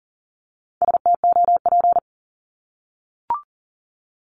Morse code, heard as a keyed tone, about 39 wpm, 700 Hz. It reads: STOP
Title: Multi-letter Phonemes 40wpm